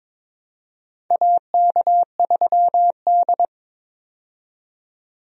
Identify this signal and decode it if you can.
Morse code, heard as a keyed tone, about 23 wpm, 700 Hz. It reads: AK3D
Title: USA Call Signs - 4 Characters - 22wpm